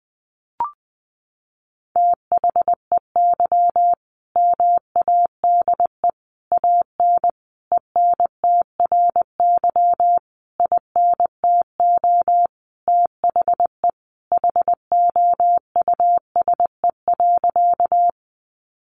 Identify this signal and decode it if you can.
Morse code, heard as a keyed tone, about 20 wpm, 700 Hz. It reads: THEY MADE AN ENTRY INTO THE HOUSE.
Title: Sentences from Top 1500 Words 20wpm